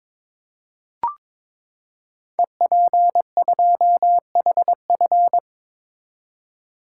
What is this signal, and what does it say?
Morse code, heard as a keyed tone, about 22 wpm, 700 Hz. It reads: EP2HF
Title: Code Groups - 5 Letter:Numbers - 22wpm